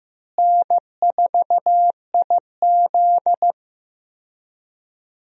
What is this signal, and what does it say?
Morse code, heard as a keyed tone, about 15 wpm, 700 Hz. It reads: N4IZ
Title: USA Call Signs - 4 Characters - Repeated 2x - 15wpm